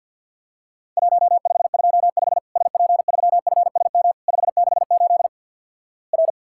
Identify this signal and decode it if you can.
Morse code, heard as a keyed tone, about 50 wpm, 700 Hz. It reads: 1525 SP3FSM 568 R
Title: Worked All Europe DX Contest - QTC Exchange - 50wpm